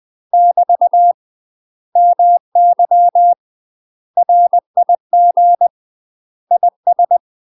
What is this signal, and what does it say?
Morse code, heard as a keyed tone, about 20 wpm, 700 Hz. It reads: <BT> MY RIG IS
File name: UK MASTER QSOs TRAINING / 20WPM / UK MASTER QSOs 0058